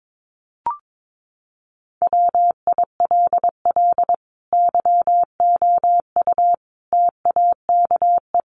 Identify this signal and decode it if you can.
Morse code, heard as a keyed tone, about 22 wpm, 700 Hz. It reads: WILL YOU TAKE
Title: Sentences from Top 900 Words 22wpm